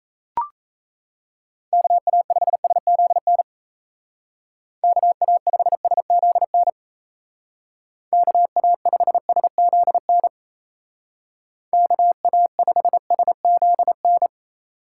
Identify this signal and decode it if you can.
Morse code, heard as a keyed tone, about 36 wpm, 700 Hz. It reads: KA5SZN KA5SZN KA5SZN KA5SZN